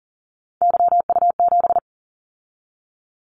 Morse code, heard as a keyed tone, about 42 wpm, 700 Hz. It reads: YU7
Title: Callsign Prefixes for Top 25 DXCC - 40wpm